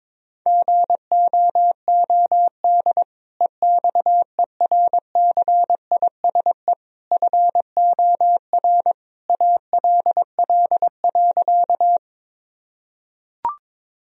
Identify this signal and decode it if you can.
Morse code, heard as a keyed tone, about 22 wpm, 700 Hz. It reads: GOOD EXERCISE FOR ALL.
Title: Sentences from Top 1000 Words - Review Part 4 - 22wpm